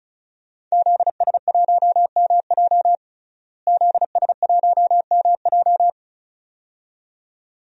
Morse code, heard as a keyed tone, about 36 wpm, 700 Hz. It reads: ZS1MJ ZS1MJ
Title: DXCC-100 Call Signs - 5 Characters - Repeated 2x - 35wpm